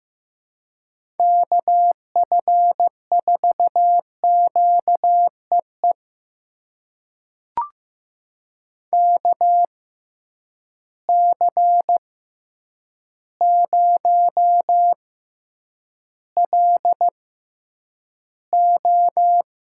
Morse code, heard as a keyed tone, about 15 wpm, spaced as for 8 wpm, 700 Hz. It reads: KF4QEE KC0LO
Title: USA Call Signs - 6 Characters - 6x Character Spacing - 15wpm